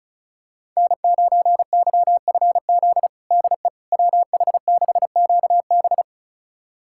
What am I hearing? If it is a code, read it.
Morse code, heard as a keyed tone, about 36 wpm, 700 Hz. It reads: N9YFZ DE WH6QB
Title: USA Callsign DE Callsign - 5 Characters - 35wpm